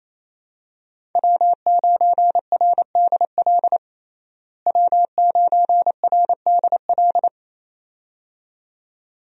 Morse code, heard as a keyed tone, about 28 wpm, 700 Hz. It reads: W9RDL W9RDL
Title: USA Call Signs - 5 Characters - Repeated 2x - 28wpm